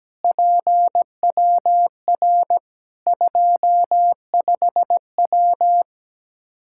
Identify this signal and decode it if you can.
Morse code, heard as a keyed tone, about 17 wpm, 700 Hz. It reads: PWR 25W
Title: Practice QSO - Exchange 2 - Info - 17wpm